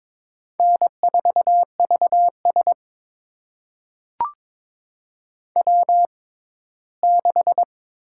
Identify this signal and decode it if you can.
Morse code, heard as a keyed tone, about 21 wpm, spaced as for 15 wpm, 700 Hz. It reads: N4VS W6
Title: USA Call Signs - 4 Characters - 6x Character Spacing - 22wpm